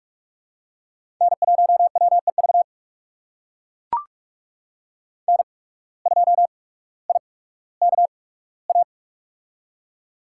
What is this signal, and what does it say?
Morse code, heard as a keyed tone, about 44 wpm, spaced as for 19 wpm, 700 Hz. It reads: N1WEV N2IKA
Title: USA Call Signs - 5 Characters - 8x Character Spacing - 45wpm